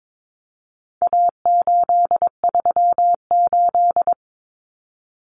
Morse code, heard as a keyed tone, about 22 wpm, 700 Hz. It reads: A838
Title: Common Cut Numbers - 4-Digit 22wpm